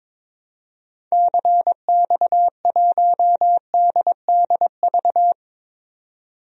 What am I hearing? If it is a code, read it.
Morse code, heard as a keyed tone, about 22 wpm, 700 Hz. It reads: CX1DDV
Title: DXCC-100 Call Signs - 6 Characters - 22wpm